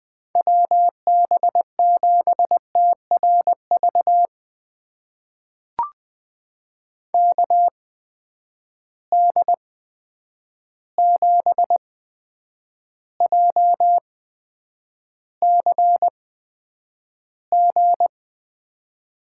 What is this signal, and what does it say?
Morse code, heard as a keyed tone, about 20 wpm, spaced as for 9 wpm, 700 Hz. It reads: WB7TRV KD7JCG
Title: USA Call Signs - 6 Characters - 8x Character Spacing - 20wpm